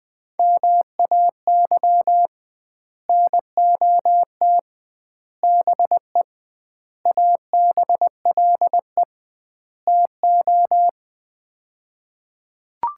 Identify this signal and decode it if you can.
Morse code, heard as a keyed tone, about 20 wpm, 700 Hz. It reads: MAY NOT BE ABLE TO
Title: 2x Word-Spacing - Sets of 5 Words - 20wpm